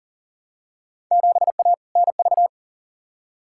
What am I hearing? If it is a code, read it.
Morse code, heard as a keyed tone, about 40 wpm, 700 Hz. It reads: 7A NV
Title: Field Day Exchanges Encore - 40wpm